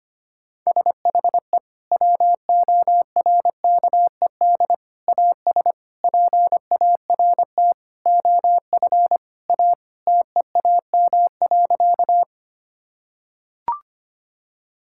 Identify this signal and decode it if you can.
Morse code, heard as a keyed tone, about 25 wpm, 700 Hz. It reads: SHE WORKED AS PART OF A TEAM.
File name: practice / 25wpm / Sentences with 4-Word Phrases - 25wpm